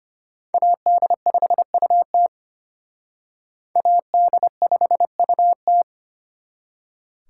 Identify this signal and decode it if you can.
Morse code, heard as a keyed tone, about 32 wpm, 700 Hz. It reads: AD5UT AD5UT